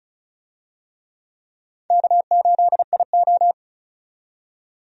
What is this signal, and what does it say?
Morse code, heard as a keyed tone, about 34 wpm, 700 Hz. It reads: K8IO
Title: USA Call Signs - 4 Characters - 35wpm